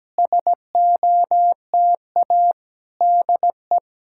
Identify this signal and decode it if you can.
Morse code, heard as a keyed tone, about 17 wpm, 700 Hz. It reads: SOTA DE
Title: SOTA QSO Practice 17wpm